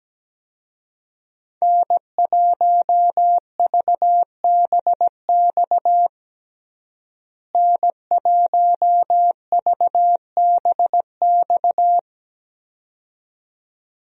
Morse code, heard as a keyed tone, about 17 wpm, 700 Hz. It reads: N1VBX N1VBX
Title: USA Call Signs - 5 Characters - Repeated 2x - 17wpm